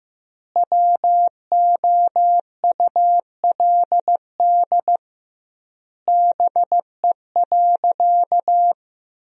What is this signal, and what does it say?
Morse code, heard as a keyed tone, about 16 wpm, 700 Hz. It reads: WOULD BE.
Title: 2x Word-Spacing - Sentences from Top 200 Words - 15wpm